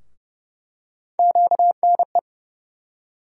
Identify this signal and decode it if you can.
Morse code, heard as a keyed tone, about 30 wpm, 700 Hz. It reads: QNE
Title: USA Call Sign Postfixes - 3 Letters 30wpm